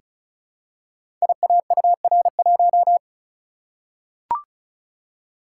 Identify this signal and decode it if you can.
Morse code, heard as a keyed tone, about 36 wpm, 700 Hz. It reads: IAUR1
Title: Code Groups - 5 Letter:Numbers - 35wpm